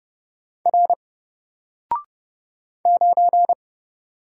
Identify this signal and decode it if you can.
Morse code, heard as a keyed tone, about 29 wpm, 700 Hz. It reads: R 9
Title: Single Character (Letter:Number:Punctuation:Prosign) 30wpm